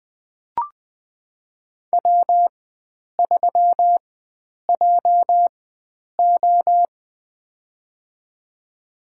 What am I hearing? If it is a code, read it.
Morse code, heard as a keyed tone, about 20 wpm, spaced as for 9 wpm, 700 Hz. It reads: W3JO
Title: USA Call Signs - 4 Characters - 4x Character Spacing - 20wpm